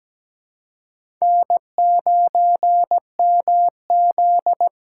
Morse code, heard as a keyed tone, about 17 wpm, 700 Hz. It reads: N9MZ
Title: USA Call Signs - 4 Characters - 2x Character Spacing - 17wpm